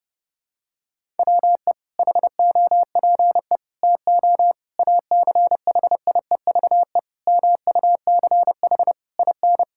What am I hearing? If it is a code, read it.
Morse code, heard as a keyed tone, about 30 wpm, 700 Hz. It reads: WE HOPE TO ACHIEVE MUCH IN
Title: Sentences from Top 1000 Words - Review Part 6 - 30wpm